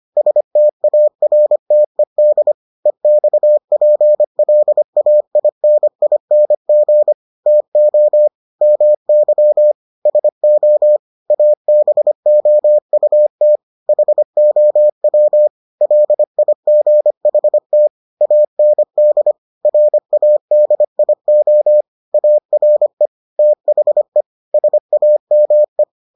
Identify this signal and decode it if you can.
Morse code, heard as a keyed tone, about 25 wpm, 600 Hz. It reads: STARTED EXPLAINING TO MY SO ABOUT HOW LIGHT AND RADIO ARE THE SAME